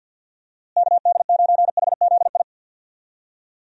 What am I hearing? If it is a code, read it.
Morse code, heard as a keyed tone, about 51 wpm, 700 Hz. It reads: KD9HZI
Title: USA Call Signs - 6 Characters - Repeated 2x - 50wpm